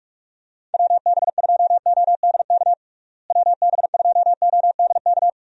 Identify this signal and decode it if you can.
Morse code, heard as a keyed tone, about 46 wpm, 700 Hz. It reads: WB2ODK WB2ODK
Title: USA Call Signs - 6 Characters - Repeated 2x - 45wpm